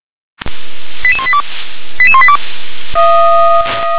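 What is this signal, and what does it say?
Five-tone selective call (ZVEI), heard as quick runs of five tones, tones of about 70 ms. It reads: 88182 88182
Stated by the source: Five-tone ZVEI selective call received by SDR.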